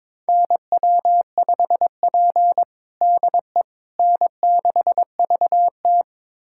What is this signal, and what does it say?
Morse code, heard as a keyed tone, about 22 wpm, 700 Hz. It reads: NW5P DE N6VT
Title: USA Callsign DE Callsign - 4 Characters - 22wpm